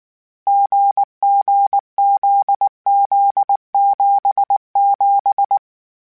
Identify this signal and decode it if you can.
Morse code, heard as a keyed tone, about 19 wpm, 800 Hz. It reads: GGZZ77
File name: CW Code Groups-mp3s / Sets mp3s / GZ GZ7 sp @18